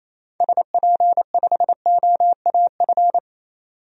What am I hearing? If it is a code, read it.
Morse code, heard as a keyed tone, about 28 wpm, 700 Hz. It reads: SP5OAF
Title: DXCC-100 Call Signs - 6 Characters - 28wpm